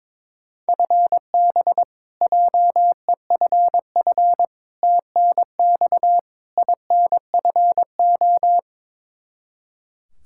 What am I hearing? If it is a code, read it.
Morse code, heard as a keyed tone, about 22 wpm, 700 Hz. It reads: FB JEFF TNX INFO
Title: Practice QSO - Exchange 3 - More Info - 22wpm